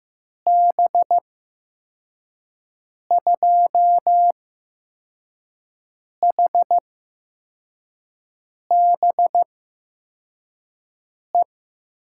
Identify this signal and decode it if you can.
Morse code, heard as a keyed tone, about 15 wpm, spaced as for 4 wpm, 700 Hz. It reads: B2HBE
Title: USA Call Signs - 6 Characters - 8x Character Spacing - 15wpm